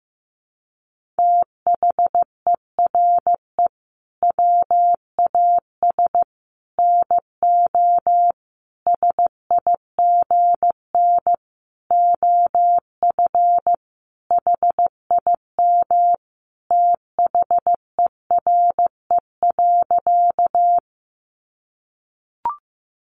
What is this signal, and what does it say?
Morse code, heard as a keyed tone, about 15 wpm, 700 Hz. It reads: THERE WAS NO SIGN OF HIM THERE.
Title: Sentences from Top 1000 Words 15wpm